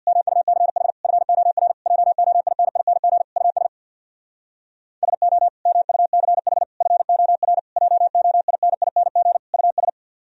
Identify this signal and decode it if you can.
Morse code, heard as a keyed tone, about 64 wpm, 700 Hz. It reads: MUCH FOR JOINING US SO MUCH FOR JOINING US